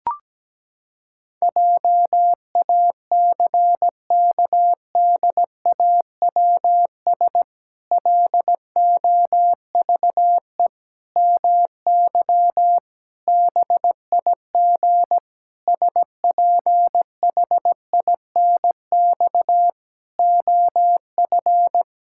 Morse code, heard as a keyed tone, about 17 wpm, 700 Hz. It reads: JACKDAWS LOVE MY BIG SPHINX OF
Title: Pangram Challenge v2 - 17wpm